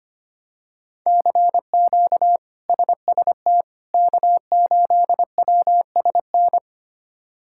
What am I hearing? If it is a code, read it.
Morse code, heard as a keyed tone, about 25 wpm, 700 Hz. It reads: CQ SST K8WSN
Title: K1USN SST QSO Practice - 25wpm